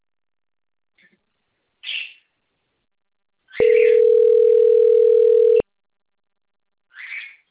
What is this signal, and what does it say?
Ringback tone (North America), 440 + 480 Hz.
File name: rg-1600-unknown- - - .56183